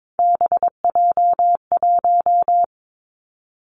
Morse code, heard as a keyed tone, about 22 wpm, 700 Hz. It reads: BJ1
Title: Three Challenging Letter-Number Combinations 22wpm